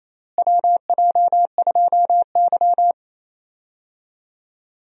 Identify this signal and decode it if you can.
Morse code, heard as a keyed tone, about 28 wpm, 700 Hz. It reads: WJ2Y
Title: USA Call Signs - 4 Characters - Repeated 2x - 28wpm